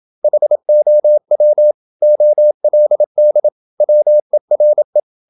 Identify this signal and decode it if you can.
Morse code, heard as a keyed tone, about 27 wpm, 600 Hz. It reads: HOW OLD WERE
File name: Never 0027wpm